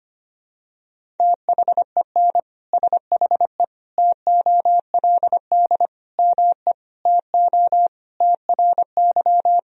Morse code, heard as a keyed tone, about 25 wpm, 700 Hz. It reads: THEN SHE TOLD ME TO TRY
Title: Sentences from Top 400 Words and Abbreviations 25wpm